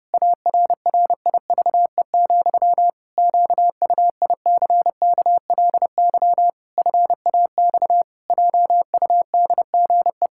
Morse code, heard as a keyed tone, about 32 wpm, 700 Hz. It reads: ARRIVE, QUICKLY FAX JUDGE
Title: Pangram Challenge v2 - 30wpm